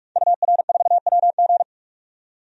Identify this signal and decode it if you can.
Morse code, heard as a keyed tone, about 43 wpm, 700 Hz. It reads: UR4WG
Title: DXCC-100 Call Signs - 5 Characters - 45wpm